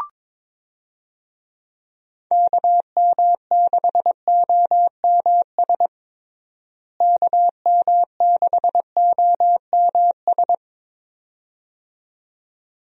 Morse code, heard as a keyed tone, about 22 wpm, 700 Hz. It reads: KM6OMS KM6OMS
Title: USA Call Signs - 6 Characters - Repeated 2x - 22wpm